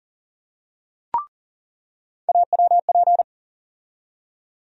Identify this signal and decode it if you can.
Morse code, heard as a keyed tone, about 40 wpm, 700 Hz. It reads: AWP
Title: USA Call Sign Postfixes - 3 Letters 40wpm